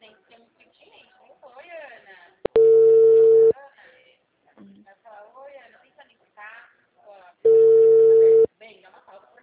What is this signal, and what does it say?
Ringback tone (China, 450 Hz), with speech.
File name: out- -2013- - - .364